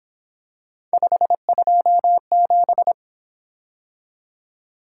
Morse code, heard as a keyed tone, about 26 wpm, 700 Hz. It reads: 527
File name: Speed-Racing - Three Letter-Number Combinations - 22wpm